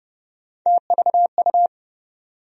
Morse code, heard as a keyed tone, about 31 wpm, 700 Hz. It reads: TVU
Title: USA Call Sign Postfixes - 3 Letters 30wpm